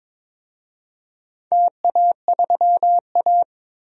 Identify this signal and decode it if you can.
Morse code, heard as a keyed tone, about 21 wpm, 700 Hz. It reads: TA3A
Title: DXCC-100 Call Signs - 4 Characters - 22wpm